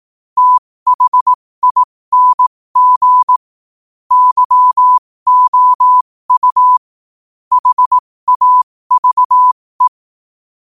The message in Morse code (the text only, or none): THING YOU HAVE